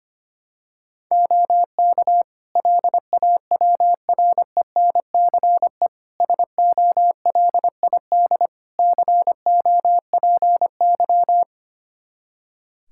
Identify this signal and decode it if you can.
Morse code, heard as a keyed tone, about 25 wpm, 700 Hz. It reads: OK LAWRENCE SOLID COPY
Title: Practice QSO - Exchange 3 - More Info - 25wpm